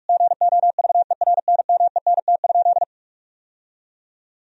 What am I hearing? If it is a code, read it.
Morse code, heard as a keyed tone, about 46 wpm, 700 Hz. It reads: GOVERNMENT?
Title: Sentences from Top 1000 Words - Review Part 3 - 45wpm